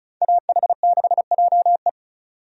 Morse code, heard as a keyed tone, about 36 wpm, 700 Hz. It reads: AH6JE
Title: USA Call Signs - 5 Characters - 35wpm